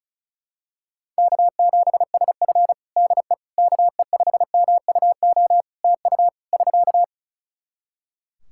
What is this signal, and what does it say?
Morse code, heard as a keyed tone, about 36 wpm, 700 Hz. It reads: K7SF DE KE5MUO TU <SK>
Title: K1USN SST QSO Practice 35wpm